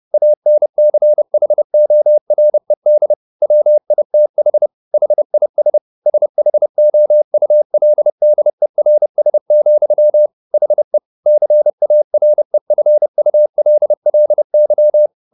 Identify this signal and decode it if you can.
Morse code, heard as a keyed tone, about 30 wpm, 600 Hz. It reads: ANCHORED WITH HIS SHOULDERS, HE CAREFULLY
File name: the old man and the sea / the old man and the sea 1 30wpm